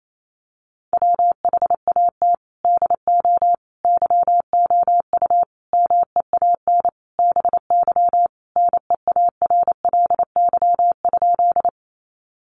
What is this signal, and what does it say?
Morse code, heard as a keyed tone, about 28 wpm, 700 Hz. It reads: WHAT DO YOU MEAN BY NEARLY?
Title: Sentences from Top 1300 Words 28wpm